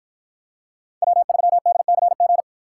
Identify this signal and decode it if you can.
Morse code, heard as a keyed tone, about 53 wpm, 700 Hz. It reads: W3DCG